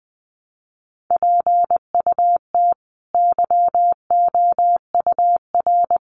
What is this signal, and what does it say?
Morse code, heard as a keyed tone, about 20 wpm, 700 Hz. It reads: PUT YOUR
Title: Sentences from Top 1100 Words 20wpm